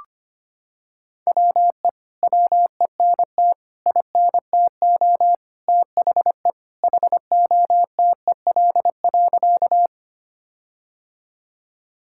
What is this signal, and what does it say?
Morse code, heard as a keyed tone, about 25 wpm, 700 Hz. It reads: WE WENT INTO THE HOTEL.
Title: Sentences from Top 1600 Words 25wpm